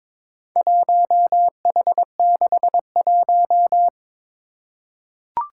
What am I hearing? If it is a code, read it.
Morse code, heard as a keyed tone, about 22 wpm, 700 Hz. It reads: 1H61
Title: Four Challenging Letter-Number Combinations 22wpm